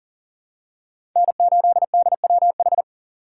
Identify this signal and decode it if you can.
Morse code, heard as a keyed tone, about 40 wpm, 700 Hz. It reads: N8DWH
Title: USA Call Signs - 5 Characters - 2x Character Spacing - 40wpm